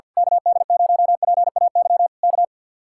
Morse code, heard as a keyed tone, about 49 wpm, 700 Hz. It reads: KD0PAY K
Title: Speed-Racing Common QSO Phrases - 50wpm